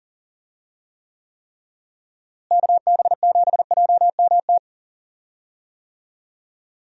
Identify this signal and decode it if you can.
Morse code, heard as a keyed tone, about 38 wpm, 700 Hz. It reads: KB7JMT
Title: USA Call Signs - 6 Characters - Repeated 3x - 40wpm